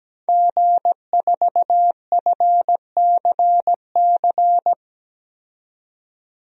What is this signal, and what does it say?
Morse code, heard as a keyed tone, about 17 wpm, 700 Hz. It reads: G4FCC
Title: DXCC-100 Call Signs - 5 Characters - 17wpm